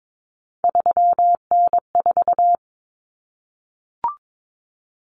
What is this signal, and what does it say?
Morse code, heard as a keyed tone, about 22 wpm, 700 Hz. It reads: 3N4
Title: Common Cut Numbers - 3-Digit 22wpm